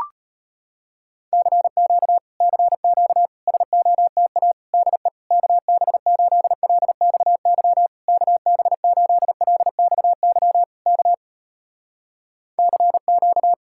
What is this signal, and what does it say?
Morse code, heard as a keyed tone, about 38 wpm, 700 Hz. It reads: CQ CQ SOTA DE KB8LXY KB8LXY K CQ